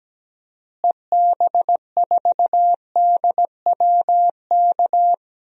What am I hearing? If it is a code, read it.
Morse code, heard as a keyed tone, about 17 wpm, 700 Hz. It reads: EB4DWK
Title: DXCC-100 Call Signs - 6 Characters - Repeated 3x - 17wpm